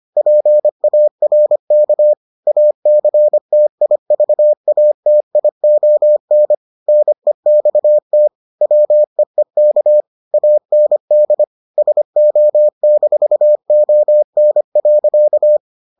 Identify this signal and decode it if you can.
Morse code, heard as a keyed tone, about 25 wpm, 600 Hz. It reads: PARK ACTIVATION NEXT WEEK AND SO-ON.